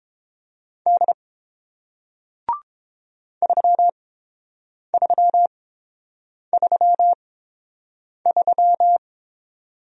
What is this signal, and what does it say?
Morse code, heard as a keyed tone, about 30 wpm, spaced as for 7 wpm, 700 Hz. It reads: D 3333